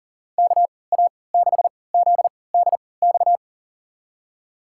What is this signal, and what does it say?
Morse code, heard as a keyed tone, about 40 wpm, spaced as for 23 wpm, 700 Hz. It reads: KA6ZDX
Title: USA Call Signs - 6 Characters - 3x Character Spacing - 40wpm